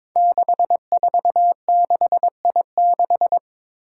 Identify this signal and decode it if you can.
Morse code, heard as a keyed tone, about 22 wpm, 700 Hz. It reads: 646I6
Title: Five Challenging Letter-Number Combinations 22wpm